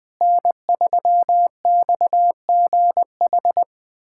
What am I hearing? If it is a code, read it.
Morse code, heard as a keyed tone, about 21 wpm, 700 Hz. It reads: N3XGH
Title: USA Callsign DE Callsign 20wpm